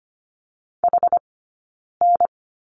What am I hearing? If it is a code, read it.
Morse code, heard as a keyed tone, about 25 wpm, 700 Hz. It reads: H N
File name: course / 25:10wpm / 092 Lesson 16d - Letter c - Two Characters 25wpm